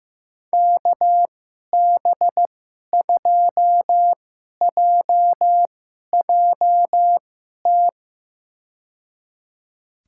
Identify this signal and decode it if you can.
Morse code, heard as a keyed tone, about 15 wpm, spaced as for 11 wpm, 700 Hz. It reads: KB2JJT
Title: USA Call Signs - 6 Characters - 2x Character Spacing - 15wpm